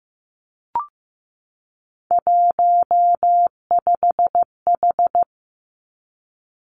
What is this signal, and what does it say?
Morse code, heard as a keyed tone, about 15 wpm, 700 Hz. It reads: 15H
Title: Three Challenging Letter-Number Combinations 15wpm